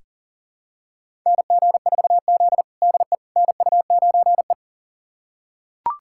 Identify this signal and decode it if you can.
Morse code, heard as a keyed tone, about 40 wpm, 700 Hz. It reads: NG4Z DE NU9E
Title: USA Callsign DE Callsign - 4 Characters - 40wpm